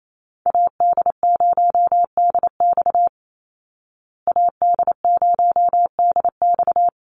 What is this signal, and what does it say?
Morse code, heard as a keyed tone, about 28 wpm, 700 Hz. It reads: AD0DX AD0DX
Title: Call Signs Repeated 2x - 28wpm